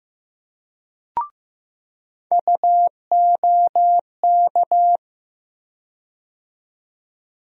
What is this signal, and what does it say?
Morse code, heard as a keyed tone, about 15 wpm, 700 Hz. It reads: UOK
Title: USA Call Sign Postfixes - 3 Letters 15wpm